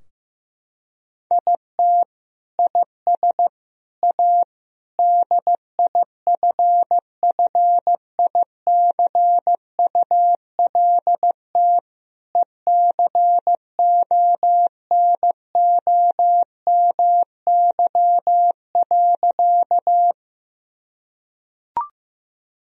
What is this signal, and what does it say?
Morse code, heard as a keyed tone, about 15 wpm, 700 Hz. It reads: IT IS A DIFFICULT ECONOMY.
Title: Sentences from Top 1000 Words - Review Part 3 - 15wpm